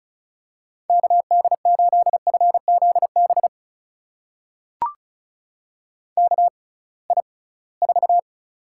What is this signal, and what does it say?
Morse code, heard as a keyed tone, about 36 wpm, spaced as for 23 wpm, 690 Hz. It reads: KD8FZB KI4
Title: USA Call Signs - 6 Characters - 6x Character Spacing - 35wpm